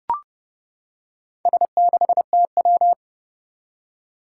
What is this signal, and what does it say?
Morse code, heard as a keyed tone, about 30 wpm, 700 Hz. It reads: S6TW
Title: Four Challenging Letter-Number Combinations 30wpm